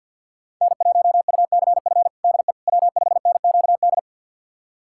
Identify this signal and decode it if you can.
Morse code, heard as a keyed tone, about 47 wpm, 700 Hz. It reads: N1UCF DE W5NQD
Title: USA Callsign DE Callsign - 5 Characters - 50wpm